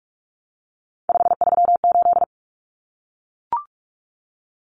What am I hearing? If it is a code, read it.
Morse code, heard as a keyed tone, about 44 wpm, 700 Hz. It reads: 538
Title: Common Cut Numbers - 3-Digit 45wpm